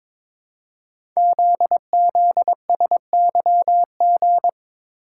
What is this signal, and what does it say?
Morse code, heard as a keyed tone, about 22 wpm, 700 Hz. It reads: ZZSYG
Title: Code Groups - 5 Letters - 22wpm